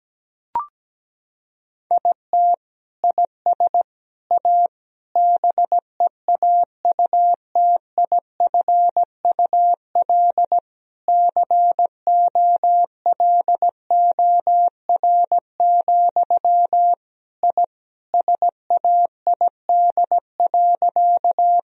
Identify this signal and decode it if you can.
Morse code, heard as a keyed tone, about 17 wpm, 700 Hz. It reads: IT IS A BEAUTIFUL COLOR, I SAID.
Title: Sentences from Top 1000 Words - Review Part 2 - 17wpm